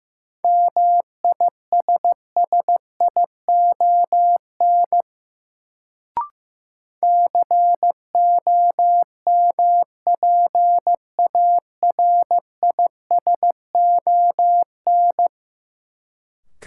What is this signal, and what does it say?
Morse code, heard as a keyed tone, about 15 wpm, 700 Hz. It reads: MISSION COMPARISON
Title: Sentences from Top 1600 Words - New Word Drill Alt - 15wpm